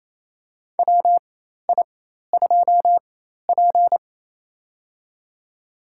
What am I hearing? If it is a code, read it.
Morse code, heard as a keyed tone, about 27 wpm, spaced as for 13 wpm, 700 Hz. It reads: WI2P
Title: USA Call Signs - 4 Characters - 4x Character Spacing - 28wpm